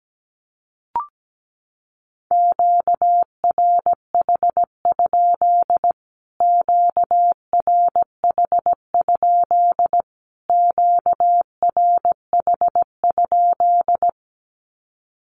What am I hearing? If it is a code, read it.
Morse code, heard as a keyed tone, about 17 wpm, 700 Hz. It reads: QRH? QRH? QRH?
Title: Q-Codes Repeated 3x 17wpm